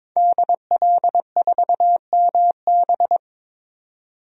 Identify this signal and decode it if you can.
Morse code, heard as a keyed tone, about 22 wpm, 700 Hz. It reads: DL4MB
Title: DXCC-100 Call Signs - 5 Characters - 22wpm